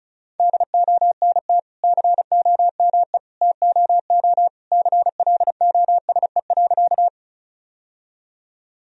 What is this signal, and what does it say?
Morse code, heard as a keyed tone, about 34 wpm, 700 Hz. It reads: DONT COME TOO CLOSE.
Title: Sentences from Top 1000 Words - Review Part 1 - 35wpm